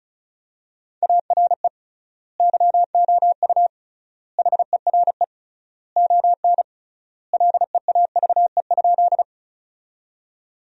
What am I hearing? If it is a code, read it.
Morse code, heard as a keyed tone, about 36 wpm, 700 Hz. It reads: ARE YOU HERE ON LEAVE?
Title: 3x Word-Spacing - Sentences from Top 400 Words - 35wpm